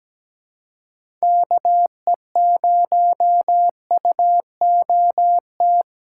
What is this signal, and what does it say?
Morse code, heard as a keyed tone, about 17 wpm, 700 Hz. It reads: KE0UOT
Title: USA Call Signs - 6 Characters - 3x Character Spacing - 17wpm